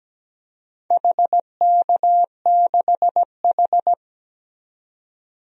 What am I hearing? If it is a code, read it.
Morse code, heard as a keyed tone, about 17 wpm, 700 Hz. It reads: HK6H
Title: DXCC-100 Call Signs - 4 Characters - 17wpm